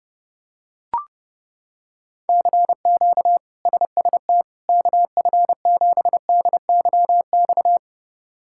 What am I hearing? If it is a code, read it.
Morse code, heard as a keyed tone, about 32 wpm, 700 Hz. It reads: CQ SST KF7DYX
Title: K1USN SST QSO Practice - 30wpm